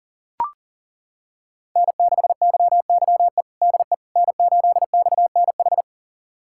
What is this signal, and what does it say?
Morse code, heard as a keyed tone, about 37 wpm, 700 Hz. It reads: N6YYE DE N8XNH
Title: USA Callsign DE Callsign - 5 Characters - 40wpm